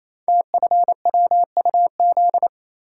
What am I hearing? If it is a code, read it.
Morse code, heard as a keyed tone, about 29 wpm, 700 Hz. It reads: TFWUZ
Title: Code Groups - 5 Letter:Numbers - 28wpm